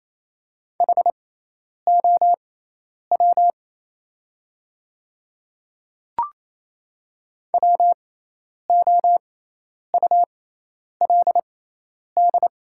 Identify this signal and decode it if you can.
Morse code, heard as a keyed tone, about 27 wpm, spaced as for 10 wpm, 700 Hz. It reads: HOW WOULD
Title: 6x Character Spacing - Top 100 Words - 28wpm